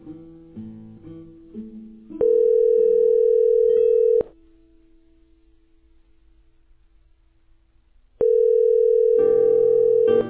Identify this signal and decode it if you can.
Ringback tone (North America), 440 + 480 Hz.